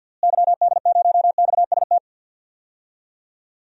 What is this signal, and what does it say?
Morse code, heard as a keyed tone, about 52 wpm, 700 Hz. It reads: YD0XST
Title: DXCC-100 Call Signs - 6 Characters - Repeated 3x - 50wpm